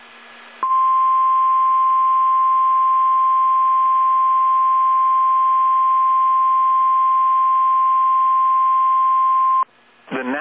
Emergency Alert System (SAME) tones (the 1050 Hz attention tone). It is heard from 0.6 to 9.7 s.